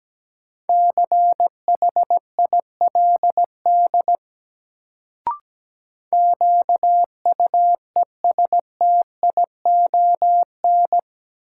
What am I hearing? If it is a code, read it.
Morse code, heard as a keyed tone, about 17 wpm, 700 Hz. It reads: CHILD QUESTION
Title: Sets of 5 Words Encore - New Word Drill - 17wpm